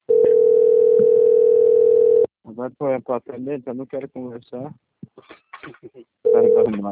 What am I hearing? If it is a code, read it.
Ringback tone (North America), 440 + 480 Hz, with speech.
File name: out- - - - - .5962